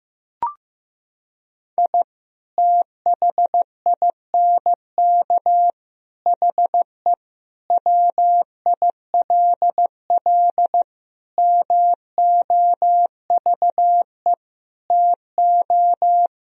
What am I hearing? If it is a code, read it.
Morse code, heard as a keyed tone, about 15 wpm, 700 Hz. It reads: I THINK HE WILL MOVE TO
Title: Sentences from Top 1000 Words - Review Part 3 - 15wpm